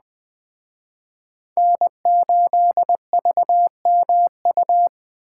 Morse code, heard as a keyed tone, about 19 wpm, 700 Hz. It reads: N8VMU